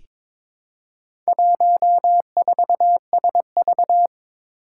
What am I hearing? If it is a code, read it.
Morse code, heard as a keyed tone, about 22 wpm, 700 Hz. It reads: 14SV
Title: Four Challenging Letter-Number Combinations 22wpm